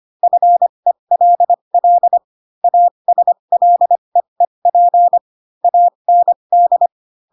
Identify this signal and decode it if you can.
Morse code, heard as a keyed tone, about 25 wpm, spaced as for 22 wpm, 700 Hz. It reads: FELL ASLEEP AND
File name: Pinocchio / Pinocchio ch7 18 25 ik1hge